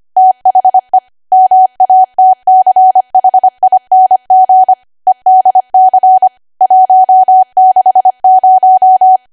Morse code, heard as a keyed tone, about 25 wpm, 750 Hz. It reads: THE MATCHING EDC 160
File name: Archive / 25 WPM / 25WPM